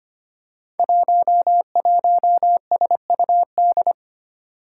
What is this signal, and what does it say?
Morse code, heard as a keyed tone, about 25 wpm, 700 Hz. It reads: 11SUD
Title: Five Challenging Letter-Number Combinations 25wpm